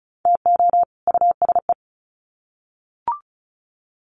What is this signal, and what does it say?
Morse code, heard as a keyed tone, about 34 wpm, 700 Hz. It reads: TO USE
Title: Sets of 2 Words 35wpm